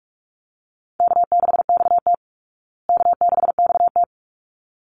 Morse code, heard as a keyed tone, about 44 wpm, 700 Hz. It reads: K6XT K6XT
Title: Easy Call Signs Repeated 2x - 45wpm